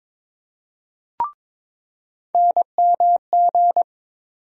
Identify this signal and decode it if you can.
Morse code, heard as a keyed tone, about 22 wpm, 700 Hz. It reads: NMG
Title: USA Call Sign Postfixes - 3 Letters 22wpm